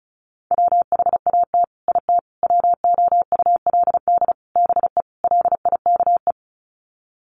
Morse code, heard as a keyed tone, about 34 wpm, 700 Hz. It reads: WHAT IT WOULD BE LIKE
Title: Sets of 5 Words Encore 35wpm